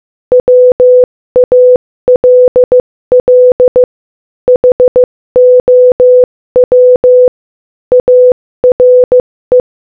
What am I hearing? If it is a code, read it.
Morse code, heard as a keyed tone, about 15 wpm, 500 Hz. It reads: WALL HOW ARE
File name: Morse-Code-Audio-Speed-15-1